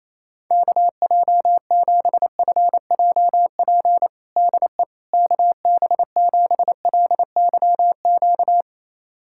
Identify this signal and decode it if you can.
Morse code, heard as a keyed tone, about 28 wpm, 700 Hz. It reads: KJ7FJP DE KB7LYQ
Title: USA Callsign DE Callsign - 6 Characters - 28wpm